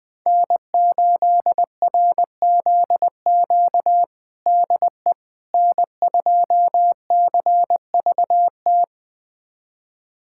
Morse code, heard as a keyed tone, about 21 wpm, 700 Hz. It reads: N8RZQ DE N2CVT
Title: USA Callsign DE Callsign - 5 Characters - 20wpm